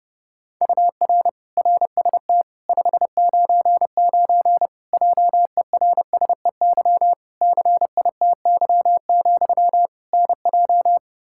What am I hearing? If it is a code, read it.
Morse code, heard as a keyed tone, about 32 wpm, 700 Hz. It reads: UR RST 599 JERSEY CITY, NJ
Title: Practice QSO - Exchange 1 - Beginning - 30wpm